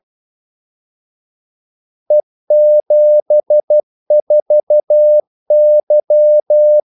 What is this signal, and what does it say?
Morse code, heard as a keyed tone, about 12 wpm, 600 Hz. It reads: E74Y
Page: 2 ekstra ordmellemrum mellem kaldesignalerne